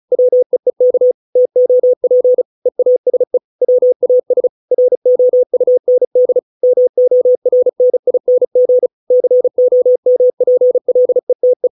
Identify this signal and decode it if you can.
Morse code, heard as a keyed tone, about 34 wpm, 500 Hz. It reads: WEEK TOP EASE WAS ROUND MORNING COMPLETE